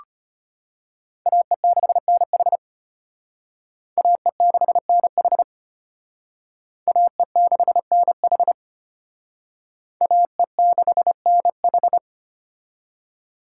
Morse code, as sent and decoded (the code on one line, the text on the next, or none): .- . -.... -. .... / .- . -.... -. .... / .- . -.... -. .... / .- . -.... -. ....
AE6NH AE6NH AE6NH AE6NH